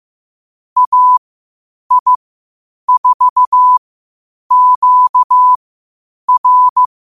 Morse code, heard as a keyed tone, about 15 wpm, spaced as for 8 wpm, 1000 Hz. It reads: AI4QR
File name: qotdfiles / 8wpm